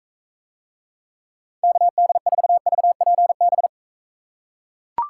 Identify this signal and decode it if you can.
Morse code, heard as a keyed tone, about 43 wpm, 700 Hz. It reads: KD4VPB